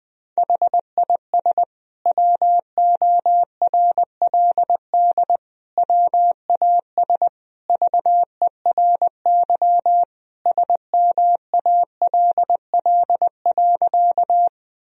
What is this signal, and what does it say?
Morse code, heard as a keyed tone, about 20 wpm, 700 Hz. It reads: HIS WORLD WAS VERY SMALL.
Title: Sentences from Top 1000 Words - Review Part 1 - 20wpm